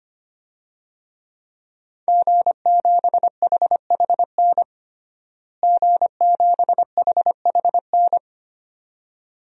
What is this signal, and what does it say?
Morse code, heard as a keyed tone, about 25 wpm, 700 Hz. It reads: G7HHN G7HHN
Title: DXCC-100 Call Signs - 5 Characters - Repeated 2x - 25wpm